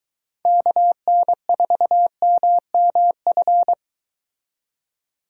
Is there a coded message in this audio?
Yes: Morse code from 0.5 to 3.7 s, heard as a keyed tone, about 23 wpm, 700 Hz. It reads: KN4MMF